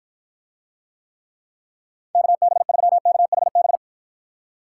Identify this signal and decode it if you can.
Morse code, heard as a keyed tone, about 50 wpm, 700 Hz. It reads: KB3KHB